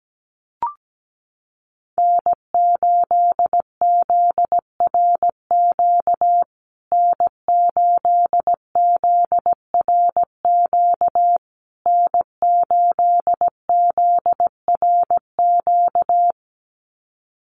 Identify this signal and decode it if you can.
Morse code, heard as a keyed tone, about 17 wpm, 700 Hz. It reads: N8ZRQ N8ZRQ N8ZRQ
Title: USA Call Signs Repeated 3x 17wpm